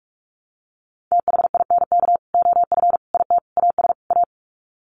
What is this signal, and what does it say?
Morse code, heard as a keyed tone, about 44 wpm, 700 Hz. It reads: THINK OF IT AS A
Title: Sets of 5 Words Encore2 45wpm